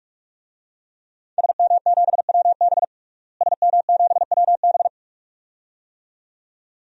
Morse code, heard as a keyed tone, about 42 wpm, 700 Hz. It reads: SM7WB SM7WB
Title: DXCC-100 Call Signs - 5 Characters - Repeated 2x - 45wpm